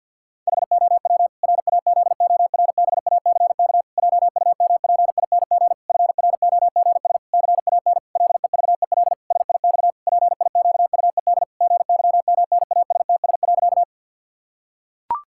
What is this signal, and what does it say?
Morse code, heard as a keyed tone, about 50 wpm, 700 Hz. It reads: HOW RAZORBACK JUMPING FROGS CAN LEVEL SIX PIQUED GYMNASTS.
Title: Pangram Challenge v2 - 50wpm